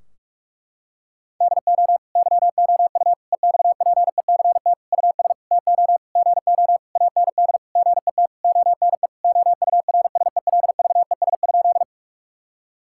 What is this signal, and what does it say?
Morse code, heard as a keyed tone, about 46 wpm, 700 Hz. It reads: DO YOU EXPECT US TO GO AND GET ONE OURSELVES?
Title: Sentences from Top 1600 Words 45wpm